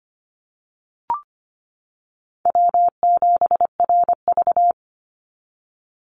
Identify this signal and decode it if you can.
Morse code, heard as a keyed tone, about 25 wpm, 700 Hz. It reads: W7RV
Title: NA SOTA Chaser Callsigns 25wpm